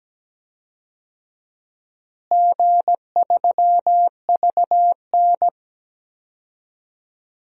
Morse code, heard as a keyed tone, about 17 wpm, 700 Hz. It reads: G3VN
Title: DXCC-100 Call Signs - 4 Characters - Repeated 2x - 17wpm